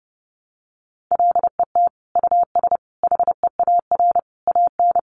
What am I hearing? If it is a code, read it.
Morse code, heard as a keyed tone, about 30 wpm, 700 Hz. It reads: LET US HEAR AN
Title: Sentences from Top 900 Words 30wpm